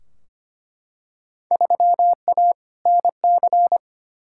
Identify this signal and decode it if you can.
Morse code, heard as a keyed tone, about 25 wpm, 700 Hz. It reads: 3A NC
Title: Field Day Exchanges - 25wpm